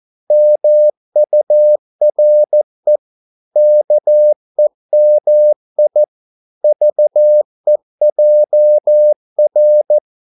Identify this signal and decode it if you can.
Morse code, heard as a keyed tone, about 14 wpm, 600 Hz. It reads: MURE KEMI VEJR
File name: Korte ord DK 14wpm